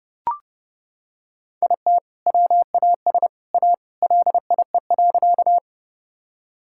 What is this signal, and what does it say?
Morse code, heard as a keyed tone, about 30 wpm, 700 Hz. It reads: IT WAS A LIE.
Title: Sentences from Top 1600 Words 30wpm